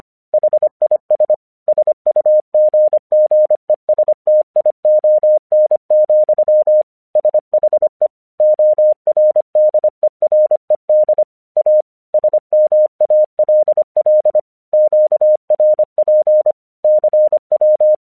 Morse code, heard as a keyed tone, about 25 wpm, 610 Hz. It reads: HIS SUGGESTION, SHE ORDERED A SMALL QRP CW
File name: 1 English 25wpm / AQC17